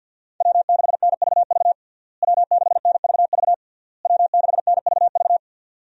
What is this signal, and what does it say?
Morse code, heard as a keyed tone, about 52 wpm, 700 Hz. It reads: W6NVV W6NVV W6NVV
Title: USA Call Signs - 5 Characters - Repeated 3x - 50wpm